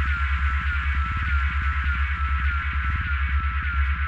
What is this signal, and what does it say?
Emergency Alert System (SAME) tones, heard as harsh buzzing bursts. It is heard from 0.3 to 3.0 s.